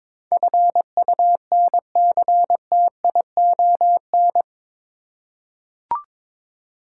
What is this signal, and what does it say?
Morse code, heard as a keyed tone, about 22 wpm, 700 Hz. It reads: FUNCTION
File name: practice / 22wpm / Eight Letter Words - 22wpm